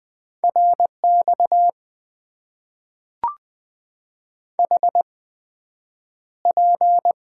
Morse code, heard as a keyed tone, about 20 wpm, spaced as for 9 wpm, 700 Hz. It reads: RX HP
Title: Top 90 QSO Elements - 8x Character Spacing - 20wpm